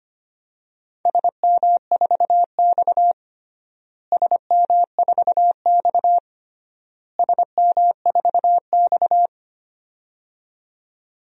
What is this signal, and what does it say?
Morse code, heard as a keyed tone, about 25 wpm, 700 Hz. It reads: SM4X SM4X SM4X
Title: DXCC-100 Call Signs - 4 Characters - Repeated 3x - 25wpm